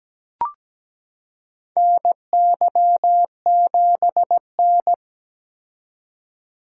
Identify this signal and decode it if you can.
Morse code, heard as a keyed tone, about 17 wpm, 700 Hz. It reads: NY7N
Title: K1USN SST QSO Practice - 17wpm